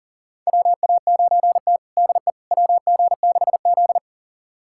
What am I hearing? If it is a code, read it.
Morse code, heard as a keyed tone, about 40 wpm, 700 Hz. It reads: WA9T DE WG6Z
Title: USA Callsign DE Callsign - 4 Characters - 40wpm